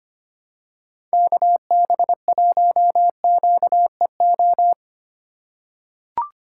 Morse code, heard as a keyed tone, about 25 wpm, 700 Hz. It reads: KB1QEO
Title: USA Call Signs - 6 Characters - 4x Character Spacing - 25wpm